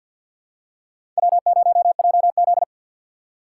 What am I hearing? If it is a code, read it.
Morse code, heard as a keyed tone, about 49 wpm, 700 Hz. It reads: W0JZ
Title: USA Call Signs - 4 Characters - 2x Character Spacing - 50wpm